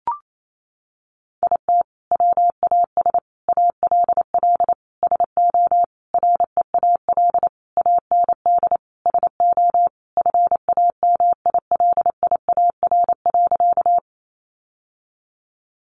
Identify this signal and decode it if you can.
Morse code, heard as a keyed tone, about 28 wpm, 700 Hz. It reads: IT WAS ALL SO REAL AND SO FAMILIAR.
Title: Sentences from Top 1000 Words 28wpm